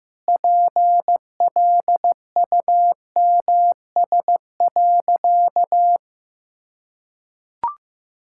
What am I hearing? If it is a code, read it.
Morse code, heard as a keyed tone, about 15 wpm, 700 Hz. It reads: PLUMS.
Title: Pangram Challenge v2 - 15wpm